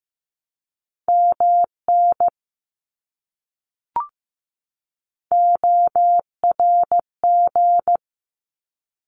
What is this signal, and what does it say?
Morse code, heard as a keyed tone, about 15 wpm, 700 Hz. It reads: MN ORG
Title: ARRL Sections 15wpm